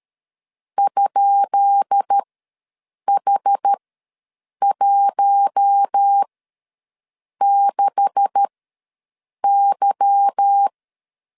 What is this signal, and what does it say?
Morse code, heard as a keyed tone, about 13 wpm, spaced as for 7 wpm, 800 Hz. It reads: ?H16Y
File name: LEZIONI / 10 SIGNS / 10B INT ws 12wpm 3527